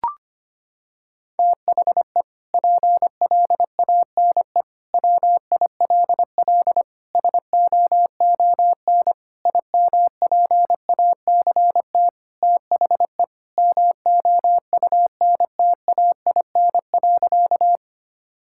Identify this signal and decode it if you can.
Morse code, heard as a keyed tone, about 25 wpm, 700 Hz. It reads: THE PLANE WILL SOON IMPACT THE MOUNTAIN.
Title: Sentences from Top 1000 Words - Review Part 6 - 25wpm